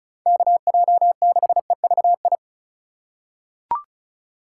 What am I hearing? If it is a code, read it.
Morse code, heard as a keyed tone, about 36 wpm, 700 Hz. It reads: KJ6EVI
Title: USA Call Signs - 6 Characters - 4x Character Spacing - 35wpm